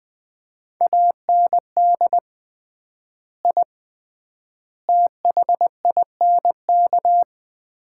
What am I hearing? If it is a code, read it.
Morse code, heard as a keyed tone, about 20 wpm, 700 Hz. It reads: AND I THINK
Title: 3x Word-Spacing - Sets of 3 Words - 20wpm